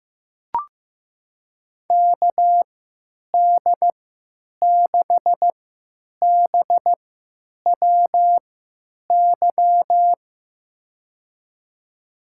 Morse code, heard as a keyed tone, about 15 wpm, spaced as for 8 wpm, 700 Hz. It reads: KD6BWY
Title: USA Call Signs - 6 Characters - 3x Character Spacing - 15wpm